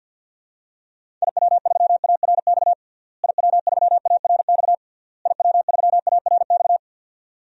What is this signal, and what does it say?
Morse code, heard as a keyed tone, about 48 wpm, 700 Hz. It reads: IW3ARX IW3ARX IW3ARX
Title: DXCC-100 Call Signs - 6 Characters - Repeated 3x - 50wpm